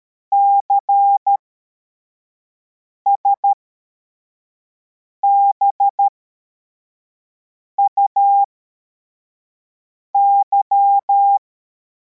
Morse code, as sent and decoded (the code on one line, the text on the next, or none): -.-. ... -... ..- -.--
CSBUY